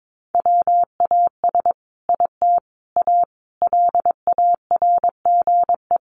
Morse code, heard as a keyed tone, about 22 wpm, 700 Hz. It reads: WAS IT A LARGE
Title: Sentences from Top 1300 Words 22wpm